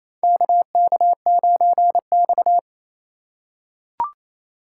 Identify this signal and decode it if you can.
Morse code, heard as a keyed tone, about 27 wpm, 700 Hz. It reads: KK9X
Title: USA Call Signs - 4 Characters - 4x Character Spacing - 28wpm